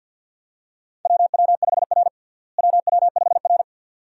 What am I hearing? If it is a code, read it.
Morse code, heard as a keyed tone, about 50 wpm, 700 Hz. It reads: WW5R WW5R
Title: USA Call Signs - 4 Characters - Repeated 2x - 50wpm